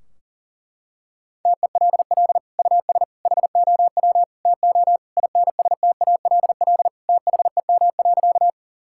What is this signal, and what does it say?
Morse code, heard as a keyed tone, about 40 wpm, 700 Hz. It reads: TELL US HOW TO INSTALL THEM.
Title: Sentences from Top 1600 Words 40wpm